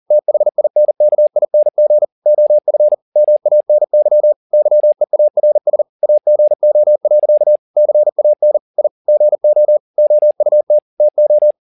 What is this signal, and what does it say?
Morse code, heard as a keyed tone, about 40 wpm, 600 Hz. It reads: THINKING OF MANY YEARS AGO. CAN I GO OUT TO